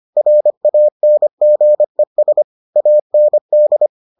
Morse code, heard as a keyed tone, about 25 wpm, 600 Hz. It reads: RANGES AND